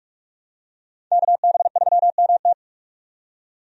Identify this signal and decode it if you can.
Morse code, heard as a keyed tone, about 43 wpm, 700 Hz. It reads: KB3MT
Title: USA Call Signs - 5 Characters - 45wpm